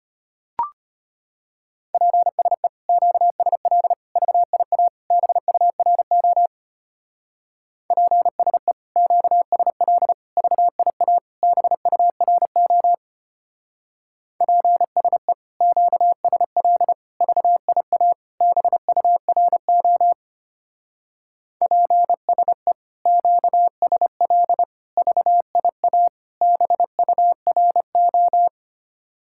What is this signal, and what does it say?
Morse code, heard as a keyed tone, about 30 wpm, 700 Hz. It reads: PSE QSL VIA BURO PSE QSL VIA BURO PSE QSL VIA BURO PSE QSL VIA BURO
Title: Speed-Racing Common QSO Phrases - 25wpm